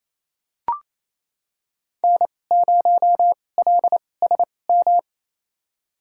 Morse code, heard as a keyed tone, about 28 wpm, spaced as for 20 wpm, 700 Hz. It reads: N0LSM
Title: USA Call Signs - 5 Characters - 2x Character Spacing - 28wpm